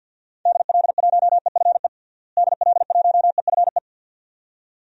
Morse code, heard as a keyed tone, about 50 wpm, 700 Hz. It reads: DL1EFE DL1EFE
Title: DXCC-100 Call Signs - 6 Characters - Repeated 2x - 50wpm